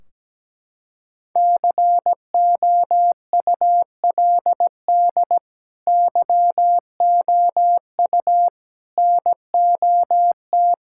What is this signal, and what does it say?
Morse code, heard as a keyed tone, about 17 wpm, 700 Hz. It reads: COULD YOU NOT
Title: Sentences from Top 1600 Words 17wpm